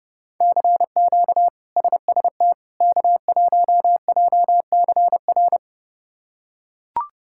Morse code, heard as a keyed tone, about 30 wpm, 700 Hz. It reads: CQ SST K1JCR
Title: K1USN SST QSO Practice - 30wpm